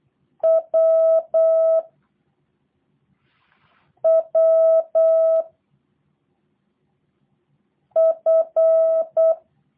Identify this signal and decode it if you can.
Morse code, heard as a keyed tone, about 8 wpm, 650 Hz. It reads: W W F
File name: jh8cbh / 2023cw / review1-4